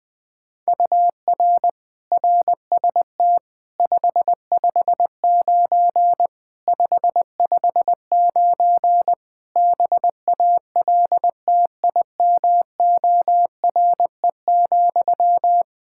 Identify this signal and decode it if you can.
Morse code, heard as a keyed tone, about 20 wpm, 700 Hz. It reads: UR RST 559 559 BALTIMORE,
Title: Practice QSO - Exchange 1 - Beginning - 20wpm